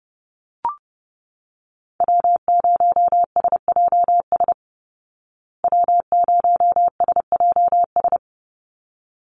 Morse code, heard as a keyed tone, about 30 wpm, 700 Hz. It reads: W0SJS W0SJS
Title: Call Signs Repeated 3x - 30wpm